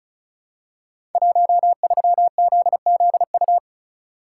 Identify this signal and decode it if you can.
Morse code, heard as a keyed tone, about 36 wpm, 700 Hz. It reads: 13ZZU
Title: Code Groups - 5 Letter:Numbers - 35wpm